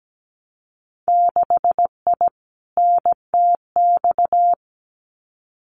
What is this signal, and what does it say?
Morse code, heard as a keyed tone, about 17 wpm, 700 Hz. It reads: 6I NTX
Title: Winter Field Day Exchanges Encore 17wpm